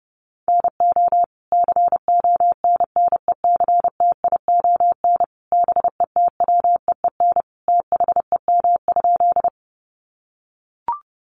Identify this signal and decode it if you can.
Morse code, heard as a keyed tone, about 30 wpm, 700 Hz. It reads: NO CONNECTION BETWEEN THEM?
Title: Sentences from Top 1300 Words 30wpm